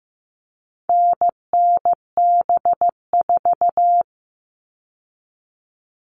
Morse code, heard as a keyed tone, about 15 wpm, 700 Hz. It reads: NNB4
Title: All Cut Numbers - 4-Digit 15wpm